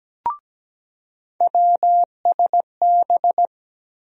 Morse code, heard as a keyed tone, about 17 wpm, 700 Hz. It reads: WSB
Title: USA Call Sign Postfixes - 3 Letters 17wpm